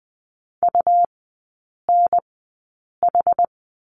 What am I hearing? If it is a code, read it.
Morse code, heard as a keyed tone, about 20 wpm, spaced as for 8 wpm, 700 Hz. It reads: UNH
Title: 133 Lesson 22e - Question Mark - Three Characters 20wpm